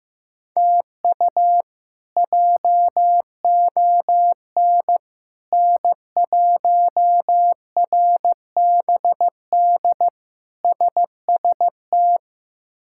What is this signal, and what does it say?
Morse code, heard as a keyed tone, about 15 wpm, 700 Hz. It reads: TU JON N1RBD SST
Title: K1USN SST QSO Practice - 15wpm